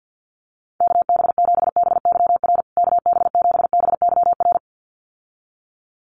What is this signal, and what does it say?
Morse code, heard as a keyed tone, about 50 wpm, 700 Hz. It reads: KB7BYR KB7BYR
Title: USA Call Signs Repeated 2x 50wpm